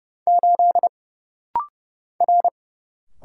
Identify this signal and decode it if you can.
Morse code, heard as a keyed tone, about 29 wpm, 700 Hz. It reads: 8 R
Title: Single Letter-Number - Rapid-Fire 30wpm